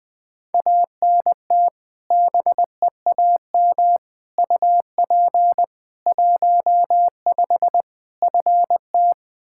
Morse code, heard as a keyed tone, about 20 wpm, 700 Hz. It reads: ANT BEAM UP 15 FT
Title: Practice QSO - Exchange 2 - Info - 20wpm